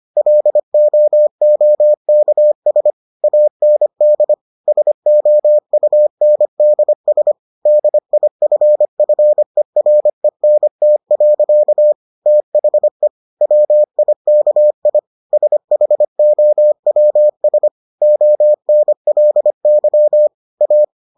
Morse code, heard as a keyed tone, about 25 wpm, 600 Hz. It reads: LOOKS AND SOUNDS DIFFERENT. THE WIKI SHOWS ONLY A